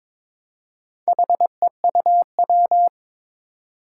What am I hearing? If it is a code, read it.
Morse code, heard as a keyed tone, about 22 wpm, 700 Hz. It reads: HEUW
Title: Four Challenging Letter-Number Combinations 22wpm